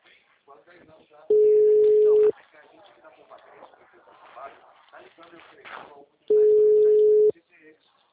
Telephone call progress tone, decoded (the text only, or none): ringback tone